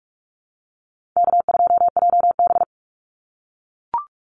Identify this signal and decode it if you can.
Morse code, heard as a keyed tone, about 44 wpm, 700 Hz. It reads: K2JB
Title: NA SOTA Activator Callsigns 45wpm